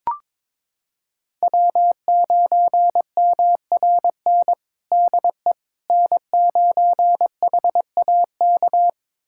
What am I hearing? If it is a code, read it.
Morse code, heard as a keyed tone, about 22 wpm, 700 Hz. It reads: W9MRN DE N9HAK
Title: USA Callsign DE Callsign - 5 Characters - 22wpm